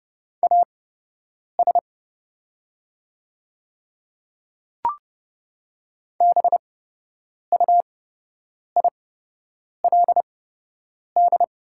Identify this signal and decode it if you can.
Morse code, heard as a keyed tone, about 29 wpm, spaced as for 8 wpm, 700 Hz. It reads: AS BUILD
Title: 8x Character Spacing - Top 300 Words - 30wpm